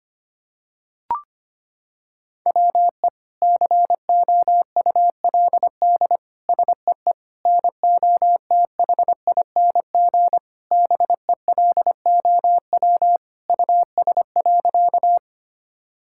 Morse code, heard as a keyed tone, about 25 wpm, 700 Hz. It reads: WE COULD SEE NOTHING BELOW US.
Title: Sentences from Top 1000 Words - Review Part 5 - 25wpm